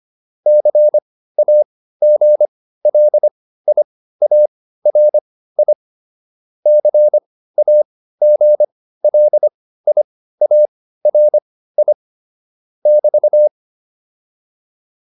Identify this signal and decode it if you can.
Morse code, heard as a keyed tone, about 25 wpm, spaced as for 15 wpm, 600 Hz. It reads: CAGLIARI CAGLIARI <BT>